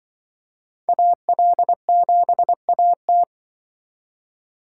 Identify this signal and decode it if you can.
Morse code, heard as a keyed tone, about 24 wpm, 700 Hz. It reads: AL7AT
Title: Speed Racing - USA 2x2 Call Signs 20wpm